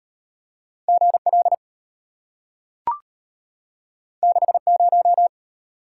Morse code, heard as a keyed tone, about 35 wpm, 700 Hz. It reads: GL 60